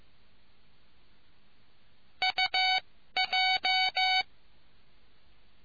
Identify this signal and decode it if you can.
Morse code, heard as a keyed tone, about 15 wpm, spaced as for 12 wpm, 750 Hz. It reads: UJ